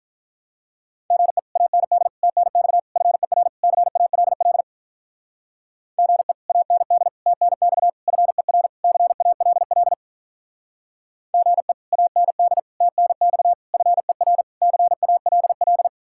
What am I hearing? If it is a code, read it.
Morse code, heard as a keyed tone, about 43 wpm, 700 Hz. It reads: GE AND TNX FER CALL GE AND TNX FER CALL GE AND TNX FER CALL
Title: Speed-Racing Common QSO Phrases Encore - 35wpm